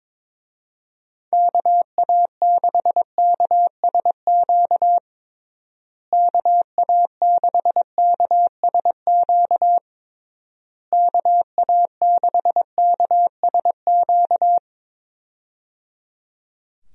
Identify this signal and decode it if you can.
Morse code, heard as a keyed tone, about 22 wpm, 700 Hz. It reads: KA6KSQ KA6KSQ KA6KSQ
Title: USA Call Signs - 6 Characters - Repeated 3x - 22wpm